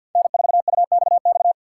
Morse code, heard as a keyed tone, about 52 wpm, 690 Hz. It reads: N4UXX
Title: USA Call Signs - 2x Character Spacing - 50wpm